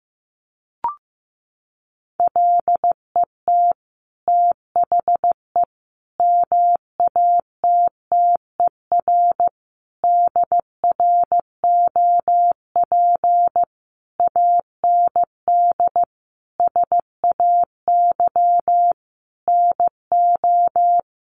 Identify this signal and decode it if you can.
Morse code, heard as a keyed tone, about 15 wpm, 700 Hz. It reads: LET THE MATTER DROP AND SAY NO
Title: Sentences from Top 1400 Words 15wpm